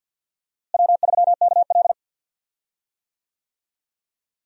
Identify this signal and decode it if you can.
Morse code, heard as a keyed tone, about 51 wpm, 700 Hz. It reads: W3KL
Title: USA Call Signs - 4 Characters - Repeated 3x - 50wpm